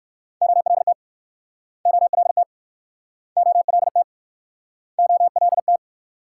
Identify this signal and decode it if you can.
Morse code, heard as a keyed tone, about 59 wpm, 700 Hz. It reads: OLT OLT OLT OLT